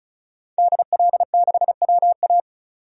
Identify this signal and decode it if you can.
Morse code, heard as a keyed tone, about 36 wpm, 700 Hz. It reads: DL6WA
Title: DXCC-100 Call Signs - 5 Characters - 35wpm